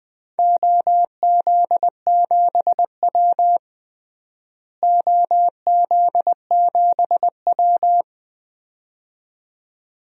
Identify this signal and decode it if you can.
Morse code, heard as a keyed tone, about 20 wpm, 700 Hz. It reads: OZ7W OZ7W
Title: DXCC-100 Call Signs - 4 Characters - Repeated 2x - 20wpm